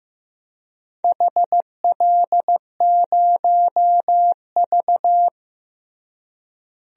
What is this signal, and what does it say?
Morse code, heard as a keyed tone, about 15 wpm, 700 Hz. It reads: HL0V
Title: DXCC-100 Call Signs - 4 Characters - Repeated 3x - 15wpm